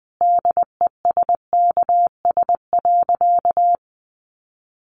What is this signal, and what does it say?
Morse code, heard as a keyed tone, about 20 wpm, 700 Hz. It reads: DESKS.
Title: Additional Easy Phrases 20wpm